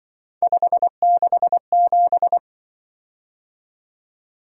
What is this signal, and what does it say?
Morse code, heard as a keyed tone, about 25 wpm, 700 Hz. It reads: 567
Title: Speed-Racing - Three Letter-Number Combinations - 20wpm